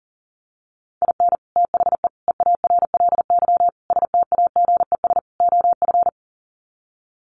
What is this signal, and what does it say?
Morse code, heard as a keyed tone, about 40 wpm, 700 Hz. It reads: IN THE EARLY STAGES OF
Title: Sets of 5 Words Encore 40wpm